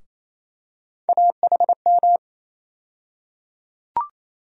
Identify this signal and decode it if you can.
Morse code, heard as a keyed tone, about 28 wpm, 700 Hz. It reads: AHM
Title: USA Call Sign Postfixes - 3 Letters 28wpm